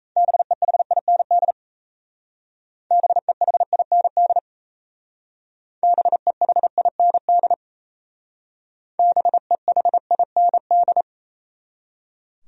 Morse code, heard as a keyed tone, about 35 wpm, 700 Hz. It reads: BEHIND BEHIND BEHIND BEHIND